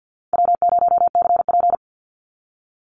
Morse code, heard as a keyed tone, about 52 wpm, 700 Hz. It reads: W0CP
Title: NA SOTA Activator Callsigns 50wpm